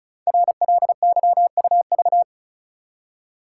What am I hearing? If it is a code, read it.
Morse code, heard as a keyed tone, about 35 wpm, 700 Hz. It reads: RLYUV
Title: Code Groups - 5 Letters - 35wpm